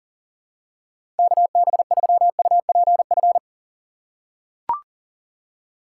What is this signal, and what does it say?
Morse code, heard as a keyed tone, about 37 wpm, 700 Hz. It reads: KB3UPF
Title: USA Call Signs - 4x Character Spacing - 40wpm